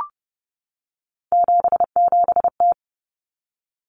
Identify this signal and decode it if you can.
Morse code, heard as a keyed tone, about 30 wpm, 700 Hz. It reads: 77T
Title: Common Cut Numbers - 3-Digit 30wpm